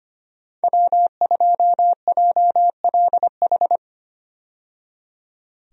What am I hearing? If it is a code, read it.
Morse code, heard as a keyed tone, about 25 wpm, 700 Hz. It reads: W2JLH
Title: USA Call Signs - 5 Characters - 25wpm